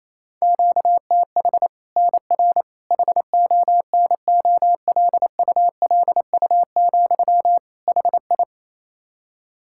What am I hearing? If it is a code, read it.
Morse code, heard as a keyed tone, about 28 wpm, 700 Hz. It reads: QTH NR HONOLULU, HI
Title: Practice QSO - Exchange 1 - Beginning - 28wpm